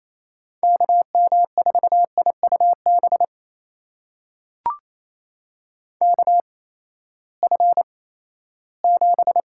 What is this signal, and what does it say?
Morse code, heard as a keyed tone, about 28 wpm, 700 Hz. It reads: K M 4 I U B K F 7
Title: USA Call Signs - 6 Characters - 8x Character Spacing - 28wpm